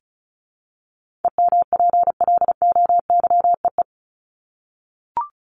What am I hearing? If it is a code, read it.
Morse code, heard as a keyed tone, about 34 wpm, 700 Hz. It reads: EMPLOYEE
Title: Top 1501-1750 Words 35wpm